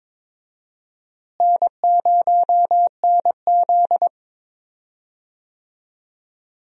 Morse code, heard as a keyed tone, about 22 wpm, 700 Hz. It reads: N0NZ
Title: USA Call Signs - 4 Characters - Repeated 2x - 22wpm